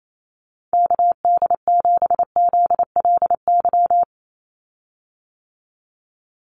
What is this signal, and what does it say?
Morse code, heard as a keyed tone, about 28 wpm, 700 Hz. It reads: KD7ZLY
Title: USA Call Signs 28wpm